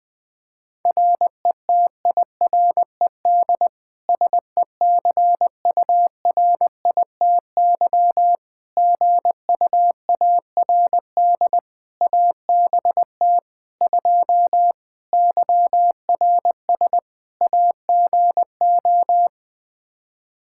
Morse code, heard as a keyed tone, about 20 wpm, 700 Hz. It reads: RETIRED SECURITY GUARD ABT 2 YRS AGO
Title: Practice QSO - Exchange 3 - More Info - 20wpm